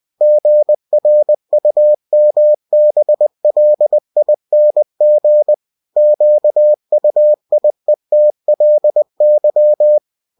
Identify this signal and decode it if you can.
Morse code, heard as a keyed tone, about 20 wpm, 600 Hz. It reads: GRUMBLING QUIETLY
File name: winnie-20wpm